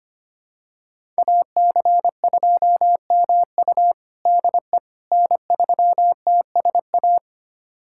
Morse code, heard as a keyed tone, about 25 wpm, 700 Hz. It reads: AC2MU DE N3TSA
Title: USA Callsign DE Callsign - 5 Characters - 25wpm